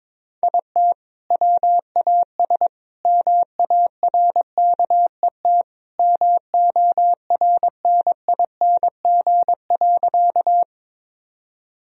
Morse code, heard as a keyed tone, about 22 wpm, 700 Hz. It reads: IT WAS MARKET MORNING.
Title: Sentences from Top 1000 Words - Review Part 3 - 22wpm